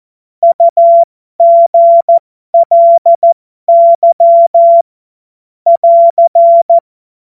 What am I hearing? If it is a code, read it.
Morse code, heard as a keyed tone, about 14 wpm, spaced as for 12 wpm, 680 Hz. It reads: UGLY <AR>
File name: Morsekurs / Witze